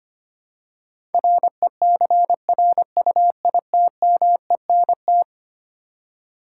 Morse code, heard as a keyed tone, about 25 wpm, 700 Hz. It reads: RECRUITMENT
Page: morse-code-ninja-practice-25wpm